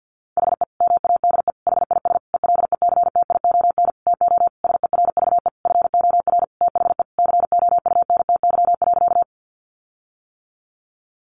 Morse code, heard as a keyed tone, about 50 wpm, 700 Hz. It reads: HE MADE HIS ELECTION TO SERVE FOR THE COUNTY.
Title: Sentences from Top 1500 Words 50wpm